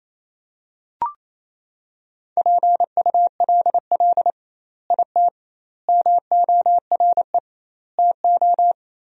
Morse code, heard as a keyed tone, about 28 wpm, 700 Hz. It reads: PULL IT MORE TO
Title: 2x Word-Spacing - Sentences from Top 400 Words - 28wpm